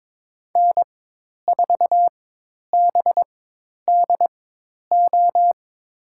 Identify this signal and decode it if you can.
Morse code, heard as a keyed tone, about 22 wpm, spaced as for 10 wpm, 700 Hz. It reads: N4BDO
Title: USA Call Signs - 4x Character Spacing - 22wpm